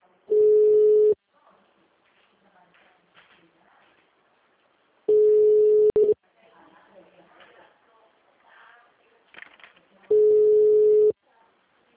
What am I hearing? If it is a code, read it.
Ringback tone (Europe, 425 Hz), with speech.